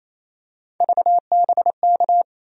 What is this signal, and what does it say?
Morse code, heard as a keyed tone, about 28 wpm, 700 Hz. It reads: VBK
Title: USA Call Sign Postfixes - 3 Letters 28wpm